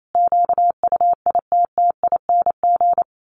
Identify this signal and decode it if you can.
Morse code, heard as a keyed tone, about 28 wpm, 700 Hz. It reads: QUITTING
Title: A Sound Of Thunder 28wpm